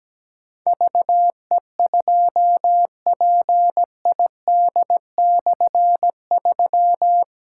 Morse code, heard as a keyed tone, about 17 wpm, 700 Hz. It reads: VE2PID/3
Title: K1USN SST QSO Practice - 17wpm